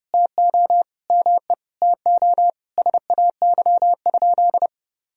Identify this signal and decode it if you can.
Morse code, heard as a keyed tone, about 30 wpm, 700 Hz. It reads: TO ME TO SAY?
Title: Sentences from Top 200 Words and Abbreviations 30wpm